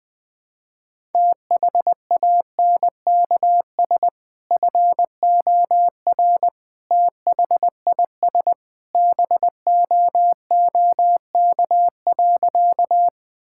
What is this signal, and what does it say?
Morse code, heard as a keyed tone, about 21 wpm, 700 Hz. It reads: THANKS FOR THIS BOOK.
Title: Sentences with 2-Word Phrases - Spoken Beginning - 20wpm